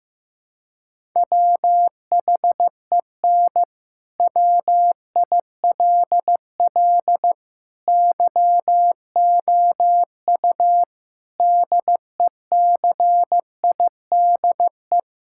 Morse code, heard as a keyed tone, about 16 wpm, 700 Hz. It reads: WHEN WILL YOU DECIDE
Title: Sentences from Top 1000 Words - Review Part 6 - 15wpm